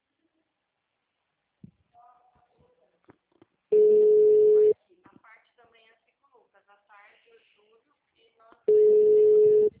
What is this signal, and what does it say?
Ringback tone (Europe, 425 Hz), with speech.
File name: out- -2038- - - .24860